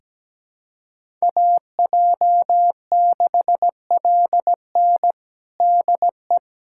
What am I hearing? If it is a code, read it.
Morse code, heard as a keyed tone, about 17 wpm, 700 Hz. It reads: AJ6LN DE
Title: USA Callsign DE Callsign - 5 Characters - 17wpm